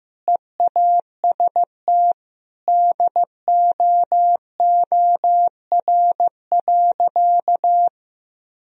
Morse code, heard as a keyed tone, about 16 wpm, 700 Hz. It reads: EAST DOOR.
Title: Sentences from Top 1000 Words - Review Part 3 - 15wpm